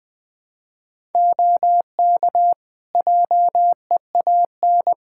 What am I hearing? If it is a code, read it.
Morse code, heard as a keyed tone, about 20 wpm, 700 Hz. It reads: OK JEAN
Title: Practice QSO - Exchange 4 - Ending - 20wpm